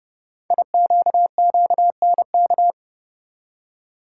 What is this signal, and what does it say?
Morse code, heard as a keyed tone, about 30 wpm, 700 Hz. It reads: IQQNK
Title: Code Groups - 5 Letters - 30wpm